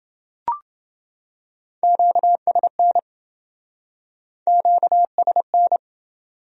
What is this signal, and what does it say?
Morse code, heard as a keyed tone, about 27 wpm, 700 Hz. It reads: QSN QSN